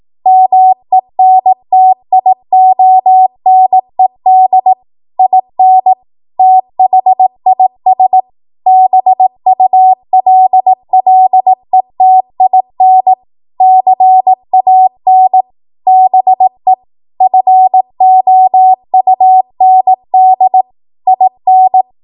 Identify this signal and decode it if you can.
Morse code, heard as a keyed tone, about 18 wpm, 750 Hz. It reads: MENTIONED IN THIS BULLETIN CAN BE FOUND IN
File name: Archive / 18 WPM / 18WPM